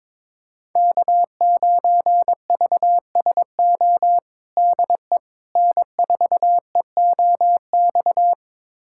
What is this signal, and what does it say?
Morse code, heard as a keyed tone, about 22 wpm, 700 Hz. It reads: K9VSO DE N4EOX
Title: USA Callsign DE Callsign - 5 Characters - 22wpm